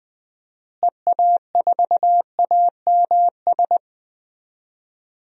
Morse code, heard as a keyed tone, about 20 wpm, 700 Hz. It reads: EA4AMS
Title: DXCC-100 Call Signs - 6 Characters - Repeated 3x - 20wpm